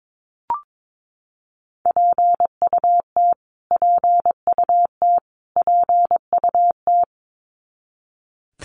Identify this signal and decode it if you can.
Morse code, heard as a keyed tone, about 22 wpm, 700 Hz. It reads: PUT PUT PUT
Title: Top 300 Words Repeated 22wpm